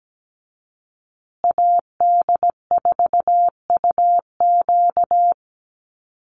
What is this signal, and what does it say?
Morse code, heard as a keyed tone, about 17 wpm, 700 Hz. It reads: AD4UQ
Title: USA Call Signs Repeated 3x 17wpm